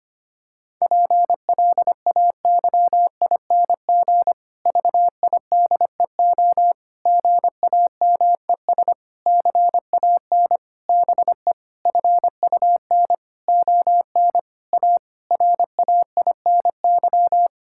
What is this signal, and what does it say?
Morse code, heard as a keyed tone, about 25 wpm, 700 Hz. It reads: PLAYING VIDEO GAMES CAN BE FUN ON A RAINY
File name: practice / 25wpm / Difficult Sentences with Hints - 25wpm